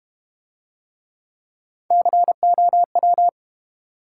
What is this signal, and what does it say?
Morse code, heard as a keyed tone, about 32 wpm, 700 Hz. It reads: COW